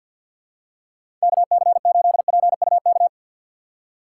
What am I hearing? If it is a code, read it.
Morse code, heard as a keyed tone, about 53 wpm, 700 Hz. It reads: KC8PUK
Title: USA Call Signs - 6x Character Spacing - 50wpm